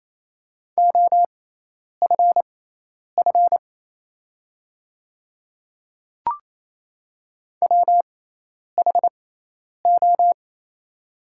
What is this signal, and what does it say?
Morse code, heard as a keyed tone, about 27 wpm, spaced as for 10 wpm, 690 Hz. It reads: OFF WHO
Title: 6x Character Spacing - Top 500 Words - 28wpm